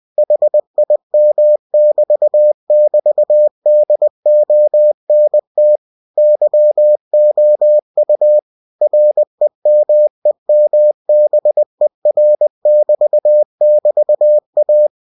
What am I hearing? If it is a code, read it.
Morse code, heard as a keyed tone, about 20 wpm, 600 Hz. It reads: HIM<BT><BT>DONT YOU REMEMBER<BT><BT>A
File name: cw-books / winnie-the-pooh / winnie-20wpm-page-0058